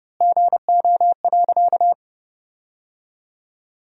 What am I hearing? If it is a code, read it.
Morse code, heard as a keyed tone, about 32 wpm, 700 Hz. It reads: GO.
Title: 4x Word-Spacing - Sentences from Top 200 Words - 30wpm